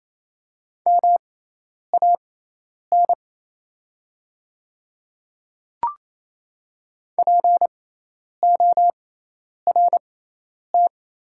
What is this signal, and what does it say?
Morse code, heard as a keyed tone, about 28 wpm, spaced as for 10 wpm, 700 Hz. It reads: MAN PORT
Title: 6x Character Spacing - Top 500 Words - 28wpm